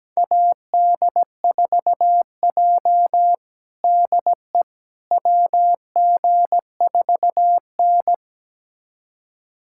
Morse code, heard as a keyed tone, about 17 wpm, 700 Hz. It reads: AD4J DE WG4N
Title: USA Callsign DE Callsign - 4 Characters - 17wpm